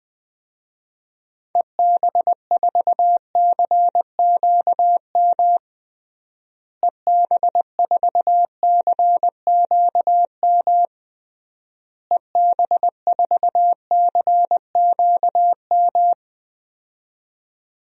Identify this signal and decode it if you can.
Morse code, heard as a keyed tone, about 20 wpm, 700 Hz. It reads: EB4CQM EB4CQM EB4CQM
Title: DXCC-100 Call Signs - 6 Characters - Repeated 3x - 20wpm